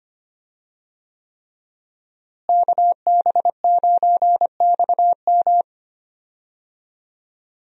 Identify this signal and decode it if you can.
Morse code, heard as a keyed tone, about 25 wpm, 700 Hz. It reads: KB9XM
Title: USA Call Signs - 5 Characters - Repeated 3x - 25wpm